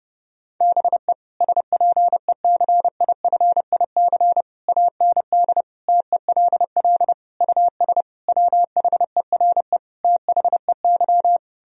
Morse code, heard as a keyed tone, about 30 wpm, 700 Hz. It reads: BE SPECIFIC AND TELL US WHERE THEY
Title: Sentences from Top 1000 Words - Review Part 3 - 30wpm